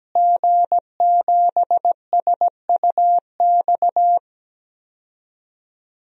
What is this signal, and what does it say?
Morse code, heard as a keyed tone, about 17 wpm, 700 Hz. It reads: G7SUX
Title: DXCC-100 Call Signs - 5 Characters - Repeated 2x - 17wpm